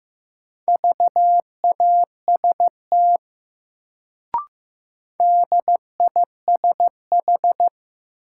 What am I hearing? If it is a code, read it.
Morse code, heard as a keyed tone, about 15 wpm, 700 Hz. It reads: VAST DISH
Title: Sentences from Top 1300 Words - New Word Drill - 15wpm